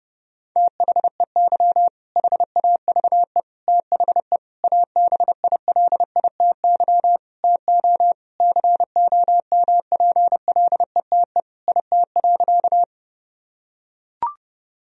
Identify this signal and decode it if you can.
Morse code, heard as a keyed tone, about 30 wpm, 700 Hz. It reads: THEY HAVE THE ABILITY TO COMPLETE IT.
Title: Sentences from Top 1000 Words - Review Part 3 - 30wpm